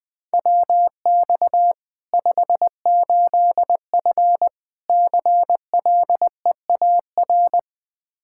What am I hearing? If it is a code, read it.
Morse code, heard as a keyed tone, about 20 wpm, 700 Hz. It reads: WX 58F CLEAR
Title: Practice QSO - Exchange 2 - Info - 20wpm